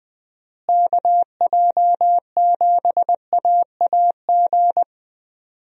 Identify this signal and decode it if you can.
Morse code, heard as a keyed tone, about 20 wpm, 700 Hz. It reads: KJ7AAG
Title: USA Call Signs - 6 Characters - Repeated 2x - 20wpm